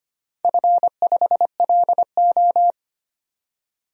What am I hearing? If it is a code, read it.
Morse code, heard as a keyed tone, about 25 wpm, 700 Hz. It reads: F5LO
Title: DXCC-100 Call Signs - 4 Characters - 25wpm